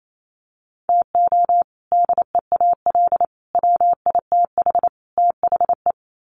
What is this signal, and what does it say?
Morse code, heard as a keyed tone, about 28 wpm, 700 Hz. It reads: TO DEAL WITH THE
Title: Sets of 4 Words 28wpm